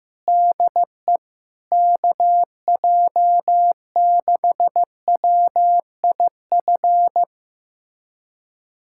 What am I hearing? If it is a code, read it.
Morse code, heard as a keyed tone, about 15 wpm, 700 Hz. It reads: DE KJ6WIF
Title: USA Callsign DE Callsign 15wpm